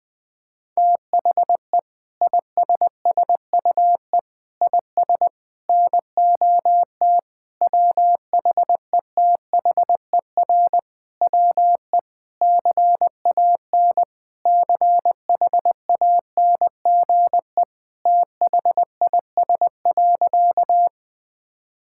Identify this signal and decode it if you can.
Morse code, heard as a keyed tone, about 20 wpm, 700 Hz. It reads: THE ISSUE IS NOT WHETHER WE CAN CHANGE THIS.
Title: Sentences from Top 1000 Words - Review Part 6 - 20wpm